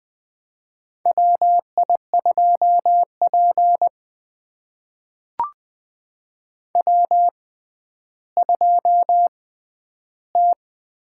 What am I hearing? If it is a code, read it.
Morse code, heard as a keyed tone, about 20 wpm, spaced as for 11 wpm, 700 Hz. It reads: WI2P W2T
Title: USA Call Signs - 4 Characters - 6x Character Spacing - 20wpm